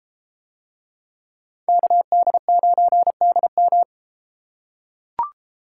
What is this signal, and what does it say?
Morse code, heard as a keyed tone, about 33 wpm, 700 Hz. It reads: KD9DM